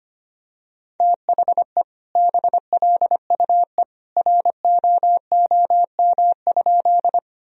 Morse code, heard as a keyed tone, about 25 wpm, 700 Hz. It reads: THE BLUE ROOM?
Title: Sentences from Top 1000 Words - Review Part 3 - 25wpm